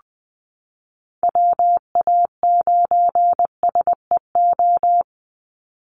Morse code, heard as a keyed tone, about 20 wpm, 700 Hz. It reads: WA9SEO
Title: Call Signs Repeated 2x - 20wpm